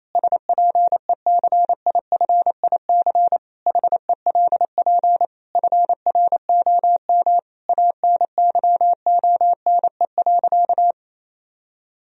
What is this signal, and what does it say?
Morse code, heard as a keyed tone, about 27 wpm, 700 Hz. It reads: SPECIFIC HELP FROM ANYONE.
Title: Sentences from Top 1000 Words - Review Part 3 - 28wpm